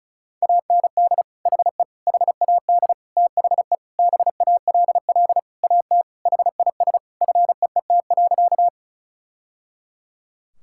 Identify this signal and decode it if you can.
Morse code, heard as a keyed tone, about 36 wpm, 690 Hz. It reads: AND HE HAD THE BALL AT HIS FEET.
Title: Sentences from Top 1000 Words - Review Part 6 - 35wpm